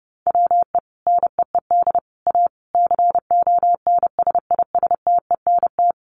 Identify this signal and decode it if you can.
Morse code, heard as a keyed tone, about 30 wpm, 700 Hz. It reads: WE NEED A CONSISTENT
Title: Sentences from Top 1500 Words 30wpm